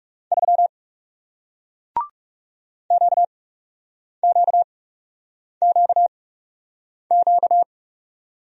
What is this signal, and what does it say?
Morse code, heard as a keyed tone, about 40 wpm, spaced as for 8 wpm, 700 Hz. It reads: 3 QQQQ